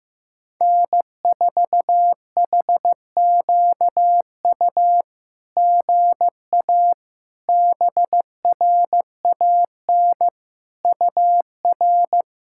Morse code, heard as a keyed tone, about 15 wpm, 700 Hz. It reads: N4HQU GA BRAN UR
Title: SOTA QSO Practice 15wpm